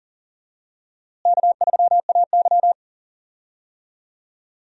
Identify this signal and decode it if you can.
Morse code, heard as a keyed tone, about 40 wpm, 700 Hz. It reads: K3AY
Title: USA Call Signs - 4 Characters - Repeated 2x - 40wpm